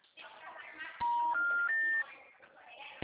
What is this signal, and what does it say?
Special information tone, with speech.